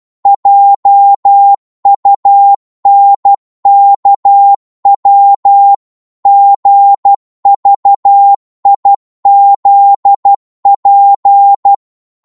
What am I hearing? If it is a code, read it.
Morse code, heard as a keyed tone, about 12 wpm, 800 Hz. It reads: JUNKW GVIZP
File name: Morse / ltr-12wpm